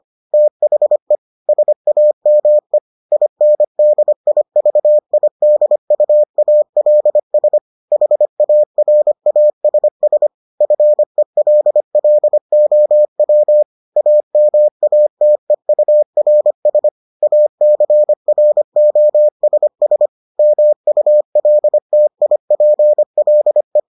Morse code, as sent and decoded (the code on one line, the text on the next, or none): - .... / ... .- -- . / .. -. -.. .. ...- .. -.. ..- .- .-.. ... / .... .- .-. .- ... ... / ..-. .-.. .-.. --- .-- / .- -- .- - . ..- .-. ... / .- -.-. .-. --- ... ... / -- ..- .-.. - .. .--. .-.. .
THE SAME INDIVIDUALS HARASS FELLOW AMATEURS ACROSS MULTIPLE